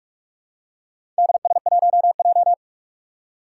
Morse code, heard as a keyed tone, about 46 wpm, 700 Hz. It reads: DS1J
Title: Four Challenging Letter-Number Combinations 45wpm